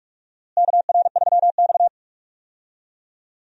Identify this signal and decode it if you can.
Morse code, heard as a keyed tone, about 46 wpm, 700 Hz. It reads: KR3X
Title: USA Call Signs - 4 Characters - Repeated 2x - 45wpm